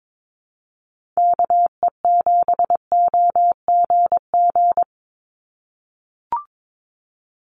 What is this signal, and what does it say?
Morse code, heard as a keyed tone, about 22 wpm, 700 Hz. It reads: KE7OGG
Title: USA Call Signs 22wpm